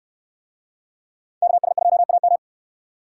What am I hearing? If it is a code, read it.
Morse code, heard as a keyed tone, about 62 wpm, 700 Hz. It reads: KS2AM